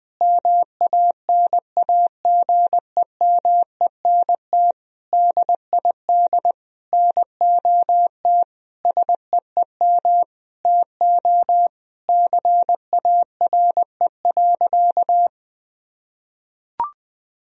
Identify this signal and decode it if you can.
Morse code, heard as a keyed tone, about 20 wpm, 700 Hz. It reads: MANAGEMENT DID NOT SEEM TO CARE.
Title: Sentences from Top 1000 Words - Review Part 4 - 20wpm